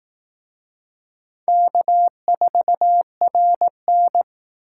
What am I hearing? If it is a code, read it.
Morse code, heard as a keyed tone, about 18 wpm, 700 Hz. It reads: K4RN
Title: Speed Racing - USA Call Signs 15wpm